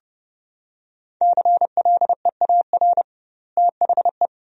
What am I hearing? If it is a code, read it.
Morse code, heard as a keyed tone, about 32 wpm, 700 Hz. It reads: CLEAR THE
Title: 2x Word-Spacing - Sentences from Top 500 Words - 30wpm